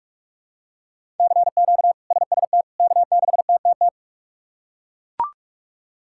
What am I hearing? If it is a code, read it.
Morse code, heard as a keyed tone, about 41 wpm, 700 Hz. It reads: CQ SST K6TTT
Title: K1USN SST QSO Practice - 45wpm